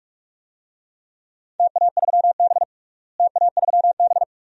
Morse code, heard as a keyed tone, about 43 wpm, 700 Hz. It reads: TA3B TA3B
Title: DXCC-100 Call Signs - 4 Characters - Repeated 2x - 45wpm